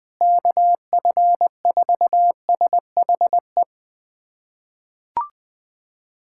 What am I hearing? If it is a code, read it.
Morse code, heard as a keyed tone, about 20 wpm, 700 Hz. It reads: KF4SHE
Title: USA Callsign DE Callsign 20wpm